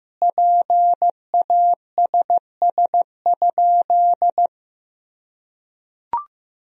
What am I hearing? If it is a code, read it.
Morse code, heard as a keyed tone, about 14 wpm, 700 Hz. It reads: PASS?
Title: Sentences from Top 400 Words and Abbreviations 15wpm